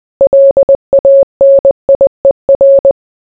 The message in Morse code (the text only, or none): LANIER